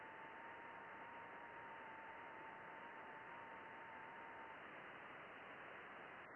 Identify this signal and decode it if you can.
FT8.
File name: contests / june-vhf-k5tr- -0200